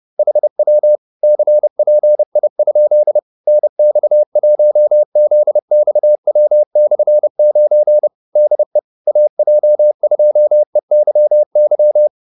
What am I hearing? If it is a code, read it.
Morse code, heard as a keyed tone, about 30 wpm, 600 Hz. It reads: HW CPI? NX1ZXW/9 DE AJ2EYY